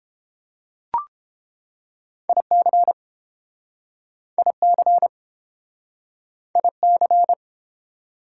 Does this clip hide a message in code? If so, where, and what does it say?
Yes: Morse code from 2.3 to 7.3 s, heard as a keyed tone, about 29 wpm, 700 Hz. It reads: IC IC IC